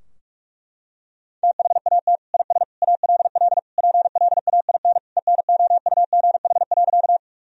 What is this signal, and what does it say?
Morse code, heard as a keyed tone, about 41 wpm, 700 Hz. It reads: THAT IS ALL PLAIN ENOUGH.
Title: Sentences from Top 1000 Words - Review Part 2 - 45wpm